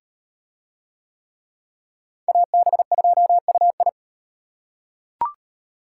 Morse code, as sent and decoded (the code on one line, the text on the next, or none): .- -... ..--- ..- ..
AB2UI